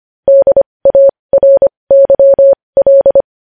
Morse code, heard as a keyed tone, about 25 wpm, spaced as for 20 wpm, 550 Hz. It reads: DARYL
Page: news-headlines-in-morse-code-at-20-wpm